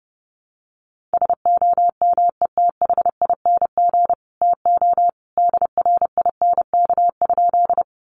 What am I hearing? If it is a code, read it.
Morse code, heard as a keyed tone, about 30 wpm, 700 Hz. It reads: SOMETHING TO DRINK?
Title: Sentences from Top 1500 Words 30wpm